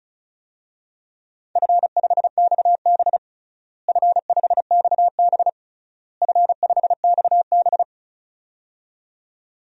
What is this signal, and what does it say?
Morse code, heard as a keyed tone, about 36 wpm, 700 Hz. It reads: F5XB F5XB F5XB
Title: DXCC-100 Call Signs - 4 Characters - Repeated 3x - 35wpm